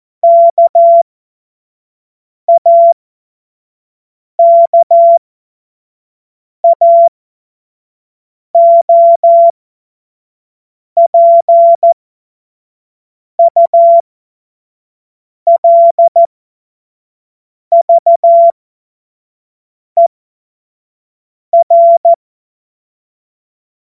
Morse code, heard as a keyed tone, about 14 wpm, spaced as for 5 wpm, 680 Hz. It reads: KAKAOPULVER
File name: Morsekurs / Witze / Witz 07. September 2021, 25 F70